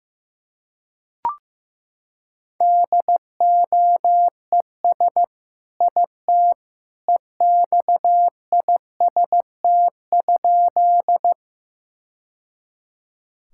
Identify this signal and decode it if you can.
Morse code, heard as a keyed tone, about 16 wpm, 700 Hz. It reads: DOES IT EXIST?
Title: Sentences from Top 1000 Words - Review Part 5 - 15wpm